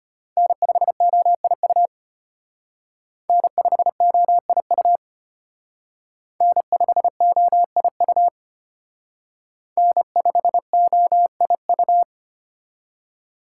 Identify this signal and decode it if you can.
Morse code, heard as a keyed tone, about 32 wpm, 700 Hz. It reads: N5OIU N5OIU N5OIU N5OIU